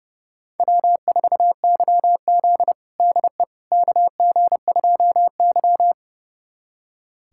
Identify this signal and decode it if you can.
Morse code, heard as a keyed tone, about 32 wpm, 700 Hz. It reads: W4YZ DE KG2Y
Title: USA Callsign DE Callsign - 4 Characters - 30wpm